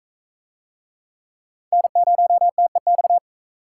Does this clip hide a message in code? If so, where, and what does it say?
Yes: Morse code from 1.7 to 3.2 s, heard as a keyed tone, about 38 wpm, 700 Hz. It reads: N0TEX